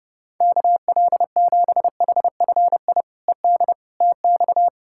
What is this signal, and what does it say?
Morse code, heard as a keyed tone, about 31 wpm, 700 Hz. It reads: KL7HFI ED TX
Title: K1USN SST QSO Practice - 30wpm